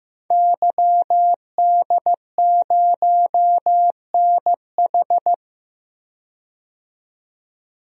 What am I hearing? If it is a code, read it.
Morse code, heard as a keyed tone, about 15 wpm, 700 Hz. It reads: YD0NH
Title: DXCC-100 Call Signs - 5 Characters - 15wpm